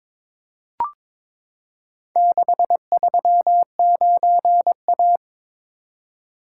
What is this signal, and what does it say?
Morse code, heard as a keyed tone, about 22 wpm, 700 Hz. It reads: 639A
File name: practice / 22wpm / Four Letter-Number Combinations - 22wpm